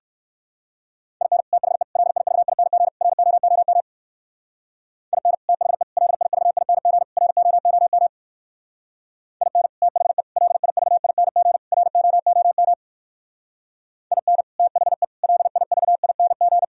Morse code, heard as a keyed tone, about 61 wpm, 700 Hz. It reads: IN THE LIVING ROOM IN THE LIVING ROOM IN THE LIVING ROOM IN THE LIVING